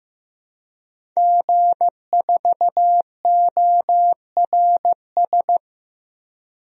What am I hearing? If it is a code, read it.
Morse code, heard as a keyed tone, about 15 wpm, 700 Hz. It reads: G4ORS
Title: DXCC-100 Call Signs - 5 Characters - 15wpm